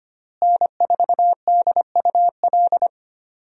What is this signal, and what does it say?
Morse code, heard as a keyed tone, about 25 wpm, 700 Hz. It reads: N4DUL
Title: USA Call Signs - 5 Characters - Repeated 3x - 25wpm